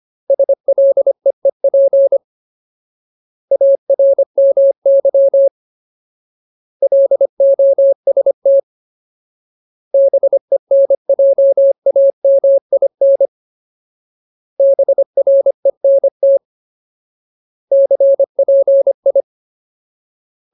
Morse code, heard as a keyed tone, about 25 wpm, 550 Hz. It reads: SLEEP ARMY LOST BENJAMIN BRENT CPI
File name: Ultimate Word List / 25WPM Ultimate